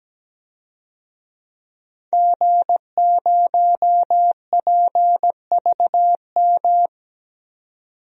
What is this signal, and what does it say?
Morse code, heard as a keyed tone, about 17 wpm, 700 Hz. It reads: G0PVM
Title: DXCC-100 Call Signs - 5 Characters - 17wpm